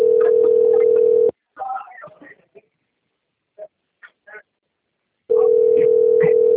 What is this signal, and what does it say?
Ringback tone (North America), 440 + 480 Hz.